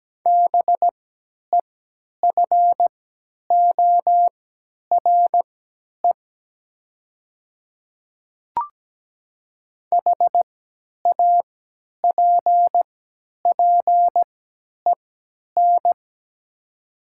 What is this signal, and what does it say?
Morse code, heard as a keyed tone, about 17 wpm, spaced as for 10 wpm, 700 Hz. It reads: BEFORE HAPPEN
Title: 3x Character Spacing - Top 500 Words - 17wpm